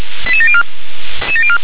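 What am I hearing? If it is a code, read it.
Five-tone selective call (ZVEI), heard as quick runs of five tones, tones of about 70 ms. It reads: 88704
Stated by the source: ZVEI five-tone selective call received by SDR.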